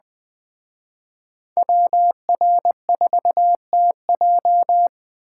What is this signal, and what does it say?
Morse code, heard as a keyed tone, about 20 wpm, 700 Hz. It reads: WR4TJ
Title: Speed Racing - USA 2x2 Call Signs 17wpm